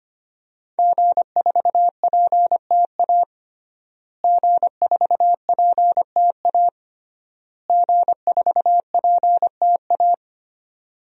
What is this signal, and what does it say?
Morse code, heard as a keyed tone, about 25 wpm, 700 Hz. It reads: G4PTA G4PTA G4PTA
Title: DXCC-100 Call Signs - 5 Characters - Repeated 3x - 25wpm